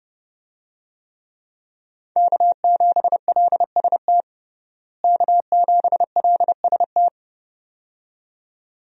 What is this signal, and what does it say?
Morse code, heard as a keyed tone, about 30 wpm, 700 Hz. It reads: K7LST K7LST
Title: USA Call Signs - 5 Characters - Repeated 2x - 30wpm